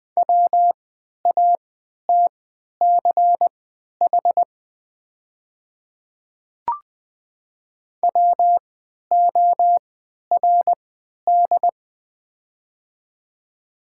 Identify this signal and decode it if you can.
Morse code, heard as a keyed tone, about 20 wpm, spaced as for 11 wpm, 690 Hz. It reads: WATCH WORD
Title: 3x Character Spacing - Top 500 Words - 20wpm